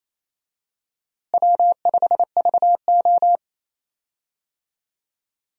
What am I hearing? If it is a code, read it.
Morse code, heard as a keyed tone, about 27 wpm, 700 Hz. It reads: W5VO
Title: USA Call Signs - 4 Characters - Repeated 2x - 28wpm